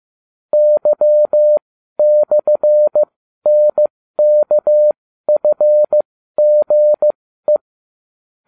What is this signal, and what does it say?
Morse code, heard as a keyed tone, about 15 wpm, spaced as for 12 wpm, 600 Hz. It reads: Y/NKFGE